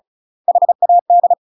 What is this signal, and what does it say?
Morse code, heard as a keyed tone, about 34 wpm, 700 Hz. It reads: HAD
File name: UK MASTER QSOs TRAINING/35WPM